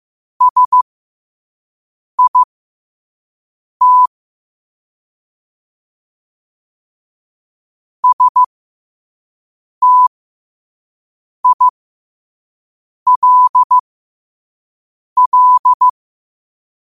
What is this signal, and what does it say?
Morse code, heard as a keyed tone, about 15 wpm, spaced as for 5 wpm, 1000 Hz. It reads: SIT STILL
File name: qotdfiles / 5wpm